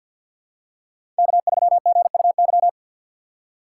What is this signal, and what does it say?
Morse code, heard as a keyed tone, about 51 wpm, 690 Hz. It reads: K3GUY
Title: USA Call Signs - 5 Characters - 2x Character Spacing - 50wpm